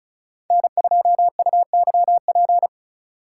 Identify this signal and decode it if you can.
Morse code, heard as a keyed tone, about 36 wpm, 700 Hz. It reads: N2UYP
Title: USA Call Signs - 5 Characters - 2x Character Spacing - 35wpm